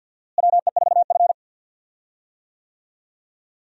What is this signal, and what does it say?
Morse code, heard as a keyed tone, about 46 wpm, 700 Hz. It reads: WE4F
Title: USA Call Signs - 4 Characters - Repeated 3x - 50wpm